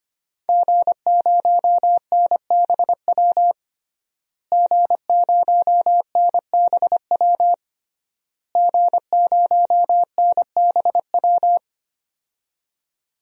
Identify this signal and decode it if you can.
Morse code, heard as a keyed tone, about 25 wpm, 700 Hz. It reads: G0NBW G0NBW G0NBW
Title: DXCC-100 Call Signs - 5 Characters - Repeated 3x - 25wpm